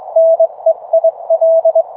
Morse code, heard as a keyed tone, about 24 wpm, spaced as for 21 wpm, 660 Hz. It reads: NEIL